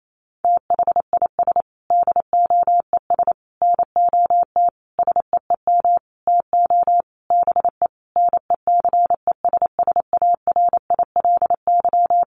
Morse code, heard as a keyed tone, about 28 wpm, 700 Hz. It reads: THIS DOES NOT SEEM TO BE NECESSARILY
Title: Sentences from Top 1400 Words 28wpm